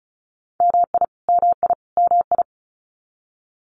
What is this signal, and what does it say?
Morse code, heard as a keyed tone, about 34 wpm, 700 Hz. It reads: MI MI MI
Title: US States Repeated 3x 35wpm